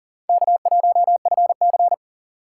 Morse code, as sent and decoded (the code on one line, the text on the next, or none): -.- .---- ..-. -.-.
K1FC